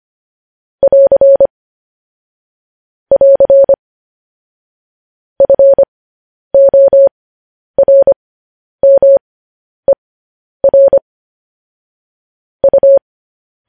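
Morse code, heard as a keyed tone, about 25 wpm, spaced as for 10 wpm, 550 Hz. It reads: <AR> <AR> FORMER U
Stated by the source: Morse news headlines at 10.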